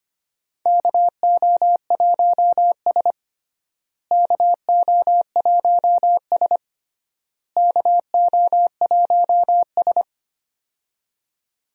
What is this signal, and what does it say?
Morse code, heard as a keyed tone, about 25 wpm, 700 Hz. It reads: KO1S KO1S KO1S
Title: USA Call Signs - 4 Characters - Repeated 3x - 25wpm